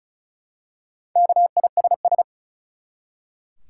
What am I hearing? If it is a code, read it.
Morse code, heard as a keyed tone, about 36 wpm, 700 Hz. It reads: KISS
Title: Four Letter Words - 35wpm